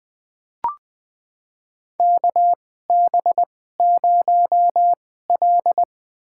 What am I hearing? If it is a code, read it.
Morse code, heard as a keyed tone, about 20 wpm, spaced as for 15 wpm, 700 Hz. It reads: KB0L
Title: USA Call Signs - 4 Characters - 2x Character Spacing - 20wpm